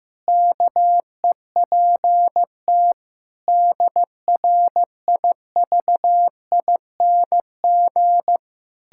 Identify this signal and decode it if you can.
Morse code, heard as a keyed tone, about 15 wpm, 700 Hz. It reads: KEPT DRIVING
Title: Pangram Challenge v2 - 15wpm